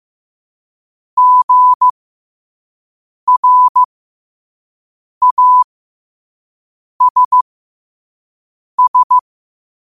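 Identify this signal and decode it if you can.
Morse code, heard as a keyed tone, about 15 wpm, spaced as for 5 wpm, 1000 Hz. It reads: GRASS
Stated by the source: Morse code quotes sent at 5 words per minute.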